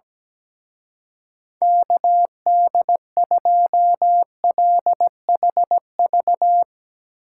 Morse code, heard as a keyed tone, about 19 wpm, 700 Hz. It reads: KD2LHV
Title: Speed Racing - USA 2x3 Call Signs 17wpm